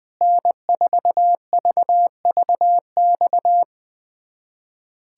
Morse code, heard as a keyed tone, about 20 wpm, 700 Hz. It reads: N4VVX
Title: USA Callsign DE Callsign 20wpm